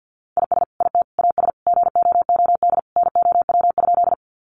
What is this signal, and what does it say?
Morse code, heard as a keyed tone, about 50 wpm, 700 Hz. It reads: IS IT AS GOOD NOW?
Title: Sentences from Top 200 Words 50wpm